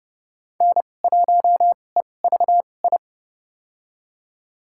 Morse code, heard as a keyed tone, about 30 wpm, spaced as for 22 wpm, 700 Hz. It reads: N1EVI
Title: USA Call Signs - 5 Characters - 2x Character Spacing - 30wpm